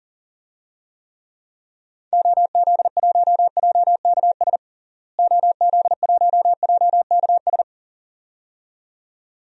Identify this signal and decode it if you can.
Morse code, heard as a keyed tone, about 40 wpm, 700 Hz. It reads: OZ1JKS OZ1JKS
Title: DXCC-100 Call Signs - 6 Characters - Repeated 2x - 40wpm